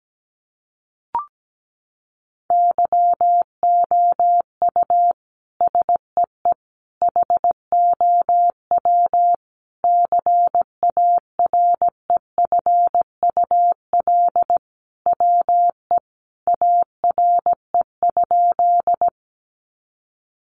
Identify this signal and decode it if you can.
Morse code, heard as a keyed tone, about 17 wpm, 700 Hz. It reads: YOU SEE HOW CAREFUL WE ARE?
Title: A Sound Of Thunder 17wpm